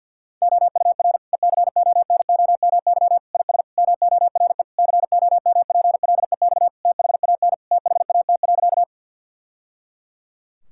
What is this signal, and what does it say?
Morse code, heard as a keyed tone, about 50 wpm, 700 Hz. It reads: OUR ECONOMY IS MORE COMPLEX THAN THAT.
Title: Sentences from Top 1000 Words - Review Part 6 - 50wpm